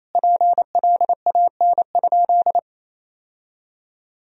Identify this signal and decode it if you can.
Morse code, heard as a keyed tone, about 28 wpm, 700 Hz. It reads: PLAN?
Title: Sentences from Top 1000 Words - Review Part 3 - 28wpm